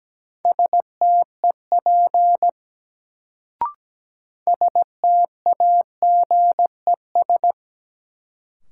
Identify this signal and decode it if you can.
Morse code, heard as a keyed tone, about 17 wpm, 700 Hz. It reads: STEP STAGES
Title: Sets of 5 Words Encore - New Word Drill - 17wpm